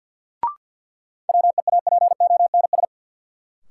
Morse code, heard as a keyed tone, about 47 wpm, 700 Hz. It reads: WEAPONS
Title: Sets of 5 Words Encore - New Word Drill - 50wpm